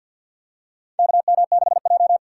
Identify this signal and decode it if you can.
Morse code, heard as a keyed tone, about 51 wpm, 700 Hz. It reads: KM6J
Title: USA Call Signs - 3x Character Spacing - 50wpm